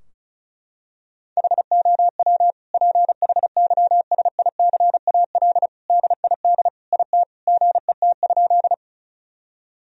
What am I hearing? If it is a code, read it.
Morse code, heard as a keyed tone, about 34 wpm, 700 Hz. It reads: HOW PHYSICAL DID IT GET?
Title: Sentences from Top 1000 Words - Review Part 4 - 35wpm